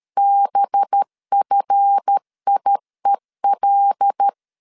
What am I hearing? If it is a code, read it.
Morse code, heard as a keyed tone, about 13 wpm, 800 Hz. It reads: BFIEL
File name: LEZIONI / 7 GF9Z / 7 GF9 12wpm 2319